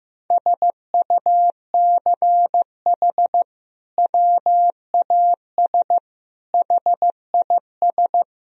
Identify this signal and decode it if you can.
Morse code, heard as a keyed tone, about 15 wpm, 700 Hz. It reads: SUCH WAS HIS
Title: Sentences from Top 1000 Words - Review Part 3 - 15wpm